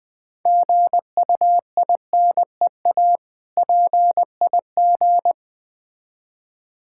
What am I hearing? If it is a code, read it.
Morse code, heard as a keyed tone, about 20 wpm, 700 Hz. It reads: GUINEA PIG
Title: Difficult Words with Hints - 20wpm